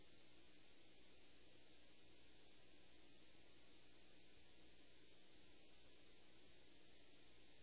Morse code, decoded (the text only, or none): none